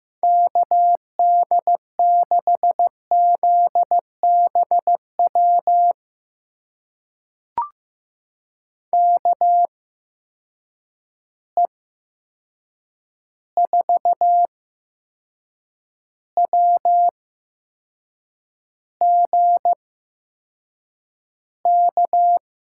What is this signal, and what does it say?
Morse code, heard as a keyed tone, about 15 wpm, spaced as for 6 wpm, 700 Hz. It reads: KD6ZBW KE4WGK
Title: USA Call Signs - 6 Characters - 8x Character Spacing - 15wpm